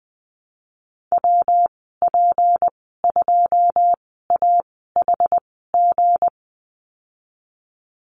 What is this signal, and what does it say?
Morse code, heard as a keyed tone, about 20 wpm, spaced as for 15 wpm, 700 Hz. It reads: WP2AHG
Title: 161 Lesson 26g - Letter g - Call Signs 20wpm